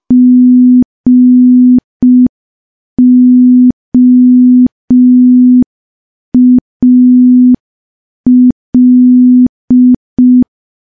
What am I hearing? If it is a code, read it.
Morse code, heard as a keyed tone, about 5 wpm, 250 Hz. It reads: GOAL
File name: morse / morse-training-05WPM